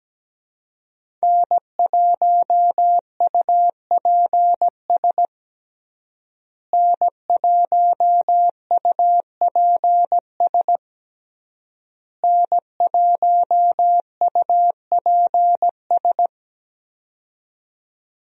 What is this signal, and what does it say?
Morse code, heard as a keyed tone, about 17 wpm, 700 Hz. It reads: N1UPS N1UPS N1UPS
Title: USA Call Signs - 5 Characters - Repeated 3x - 17wpm